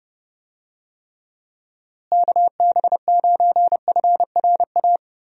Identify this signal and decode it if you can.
Morse code, heard as a keyed tone, about 30 wpm, 700 Hz. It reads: KB9FRA
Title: USA Call Signs - 6 Characters - Repeated 3x - 30wpm